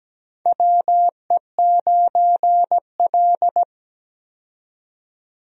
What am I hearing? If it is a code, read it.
Morse code, heard as a keyed tone, about 17 wpm, 700 Hz. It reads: WE9L
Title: USA Call Signs - 4 Characters - 17wpm